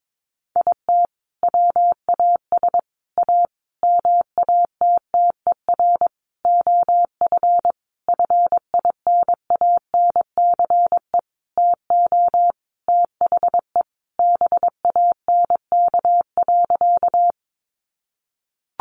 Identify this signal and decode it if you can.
Morse code, heard as a keyed tone, about 22 wpm, 700 Hz. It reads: IT WAS A MATTER OF FINANCE TO THE BANK.
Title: Sentences from Top 1500 Words 22wpm